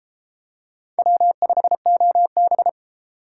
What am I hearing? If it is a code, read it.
Morse code, heard as a keyed tone, about 31 wpm, 700 Hz. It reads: W5OB